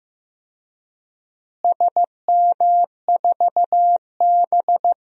Morse code, heard as a keyed tone, about 15 wpm, 700 Hz. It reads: SM4B
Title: DXCC-100 Call Signs - 4 Characters - Repeated 3x - 15wpm